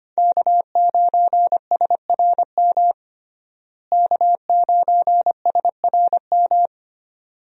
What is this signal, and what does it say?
Morse code, heard as a keyed tone, about 25 wpm, 700 Hz. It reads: K9SRM K9SRM
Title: USA Call Signs - 5 Characters - Repeated 2x - 25wpm